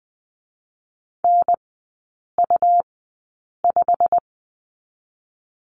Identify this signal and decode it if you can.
Morse code, heard as a keyed tone, about 19 wpm, 700 Hz. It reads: N U 5
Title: 114 Lesson 19e - Letter w - Three Characters 20wpm